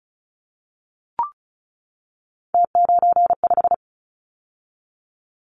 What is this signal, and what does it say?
Morse code, heard as a keyed tone, about 36 wpm, 700 Hz. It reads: T95
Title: Callsign Prefixes for Top 50 DXCC - 35wpm